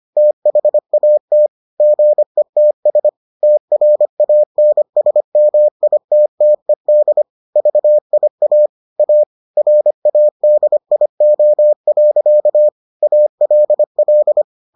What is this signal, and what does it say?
Morse code, heard as a keyed tone, about 25 wpm, 600 Hz. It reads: THAT GETS TRANSMITTED VIA A RADIO. ALL